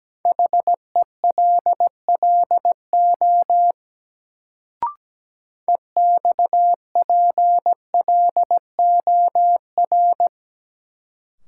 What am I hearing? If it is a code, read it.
Morse code, heard as a keyed tone, about 17 wpm, 700 Hz. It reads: HELLO EXPLOR
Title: Sentences from Top 1100 Words - New Word Drill - 17wpm